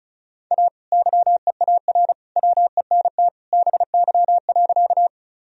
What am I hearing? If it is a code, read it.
Morse code, heard as a keyed tone, about 34 wpm, 700 Hz. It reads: A YEAR WENT BY.
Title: Sentences from Top 1000 Words - Review Part 1 - 35wpm